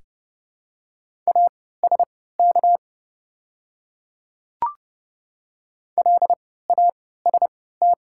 Morse code, heard as a keyed tone, about 30 wpm, spaced as for 17 wpm, 690 Hz. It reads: ASK LAST
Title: 3x Character Spacing - Top 500 Words - 30wpm